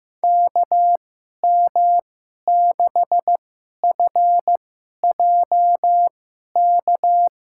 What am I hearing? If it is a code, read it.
Morse code, heard as a keyed tone, about 15 wpm, spaced as for 11 wpm, 700 Hz. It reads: KM6FJK
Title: USA Call Signs - 6 Characters - 2x Character Spacing - 15wpm